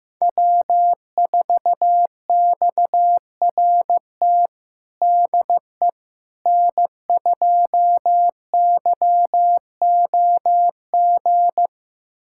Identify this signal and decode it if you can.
Morse code, heard as a keyed tone, about 16 wpm, 700 Hz. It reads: W4XRT DE N2YOG
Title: USA Callsign DE Callsign 15wpm